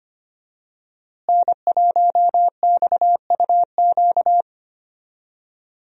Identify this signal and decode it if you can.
Morse code, heard as a keyed tone, about 25 wpm, 700 Hz. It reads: N1XUQ
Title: USA Call Signs - 5 Characters - Repeated 3x - 25wpm